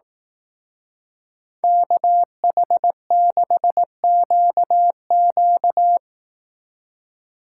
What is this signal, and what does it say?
Morse code, heard as a keyed tone, about 18 wpm, 700 Hz. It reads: KH6QQ
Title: Speed Racing - USA 2x2 Call Signs 15wpm